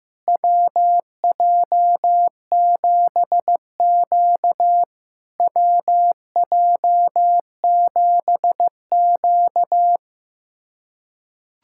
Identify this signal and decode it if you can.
Morse code, heard as a keyed tone, about 15 wpm, 700 Hz. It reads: WJ7Q WJ7Q
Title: SOTA QSO Practice 15wpm